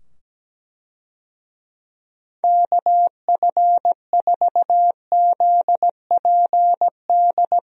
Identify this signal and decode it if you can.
Morse code, heard as a keyed tone, about 17 wpm, 700 Hz. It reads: KF4ZPD
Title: USA Call Signs - 6 Characters - 17wpm